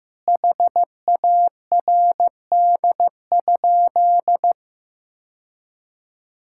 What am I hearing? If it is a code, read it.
Morse code, heard as a keyed tone, about 15 wpm, 700 Hz. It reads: HARD?
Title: Sentences from Top 1000 Words - Review Part 5 - 15wpm